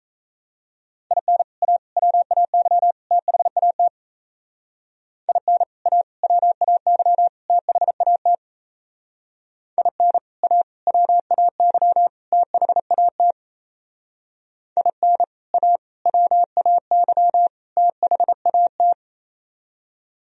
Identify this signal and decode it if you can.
Morse code, heard as a keyed tone, about 35 wpm, 700 Hz. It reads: IN A WAY THAT IN A WAY THAT IN A WAY THAT IN A WAY THAT